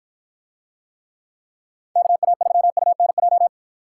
Morse code, heard as a keyed tone, about 53 wpm, 700 Hz. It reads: KA3UNJ